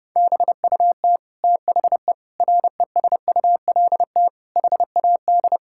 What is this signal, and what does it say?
Morse code, heard as a keyed tone, about 30 wpm, 700 Hz. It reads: BUT THE RESULT HAD
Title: Sentences from Top 1000 Words - Review Part 5 - 30wpm